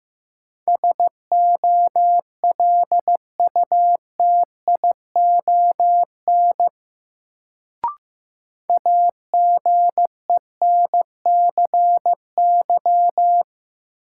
Morse code, heard as a keyed tone, about 15 wpm, 700 Hz. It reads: SOLUTION AGENCY
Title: Sentences from Top 1300 Words - New Word Drill - 15wpm